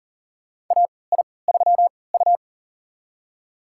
Morse code, heard as a keyed tone, about 39 wpm, spaced as for 22 wpm, 700 Hz. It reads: AI3U
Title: USA Call Signs - 4 Characters - 3x Character Spacing - 40wpm